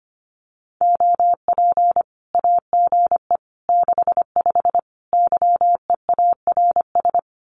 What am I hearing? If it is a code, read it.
Morse code, heard as a keyed tone, about 25 wpm, 700 Hz. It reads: OP AGE 65 YEARS
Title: Common QSO Phrases 25wpm